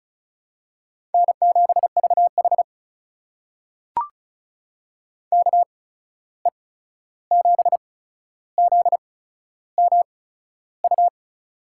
Morse code, heard as a keyed tone, about 36 wpm, spaced as for 13 wpm, 700 Hz. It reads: N7VH KE7ZMU
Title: USA Call Signs - 8x Character Spacing - 35wpm